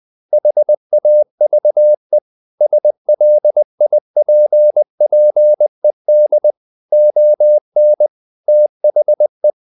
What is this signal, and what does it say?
Morse code, heard as a keyed tone, about 20 wpm, 600 Hz. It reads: HAVE SLIPPED ON THE
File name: the old man and the sea / the old man and the sea part 1 20wpm